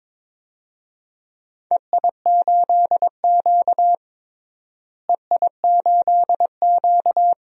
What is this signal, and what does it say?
Morse code, heard as a keyed tone, about 22 wpm, 700 Hz. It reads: EI8Q EI8Q
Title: DXCC-100 Call Signs - 4 Characters - Repeated 2x - 22wpm